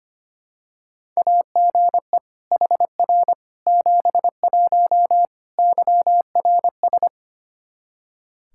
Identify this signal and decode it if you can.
Morse code, heard as a keyed tone, about 25 wpm, 700 Hz. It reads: AGE HR 71 YRS
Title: Practice QSO - Exchange 3 - More Info - 25wpm